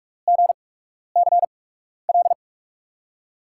Morse code, heard as a keyed tone, about 44 wpm, spaced as for 12 wpm, 700 Hz. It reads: GCL
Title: USA Call Signs - 5 Characters - 8x Character Spacing - 45wpm